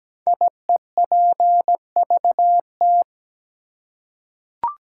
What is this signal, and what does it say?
Morse code, heard as a keyed tone, about 17 wpm, 700 Hz. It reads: IEPVT
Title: Code Groups - 5 Letters - 17wpm